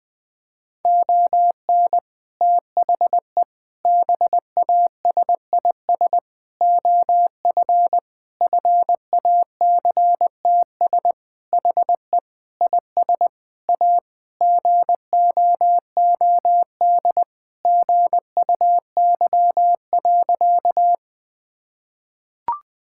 Morse code, heard as a keyed tone, about 20 wpm, 700 Hz. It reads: ON THE BASIS OF FACTS HE IS A GOOD GUY.
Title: Sentences with 3-Word Phrases - 20wpm